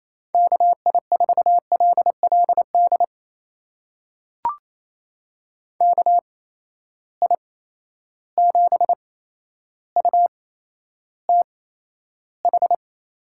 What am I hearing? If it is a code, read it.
Morse code, heard as a keyed tone, about 28 wpm, spaced as for 12 wpm, 700 Hz. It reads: KI4LLD KI7UTH
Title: USA Call Signs - 6 Characters - 8x Character Spacing - 28wpm